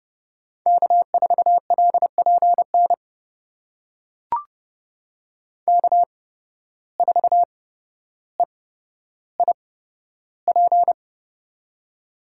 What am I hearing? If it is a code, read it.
Morse code, heard as a keyed tone, about 31 wpm, spaced as for 13 wpm, 700 Hz. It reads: K4LPN K4EIP
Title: USA Call Signs - 5 Characters - 8x Character Spacing - 30wpm